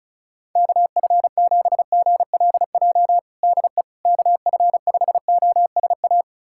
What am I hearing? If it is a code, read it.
Morse code, heard as a keyed tone, about 34 wpm, 700 Hz. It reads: KF7GLJ DE KF5OSA
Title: USA Callsign DE Callsign - 6 Characters - 35wpm